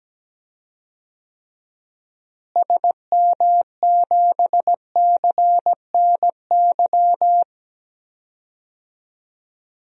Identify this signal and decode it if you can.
Morse code, heard as a keyed tone, about 17 wpm, 700 Hz. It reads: SM7CNY
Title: DXCC-100 Call Signs - 6 Characters - Repeated 3x - 17wpm